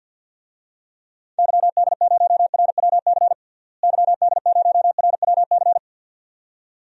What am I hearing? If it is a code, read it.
Morse code, heard as a keyed tone, about 52 wpm, 700 Hz. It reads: YD0RWC YD0RWC
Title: DXCC-100 Call Signs - 6 Characters - Repeated 2x - 50wpm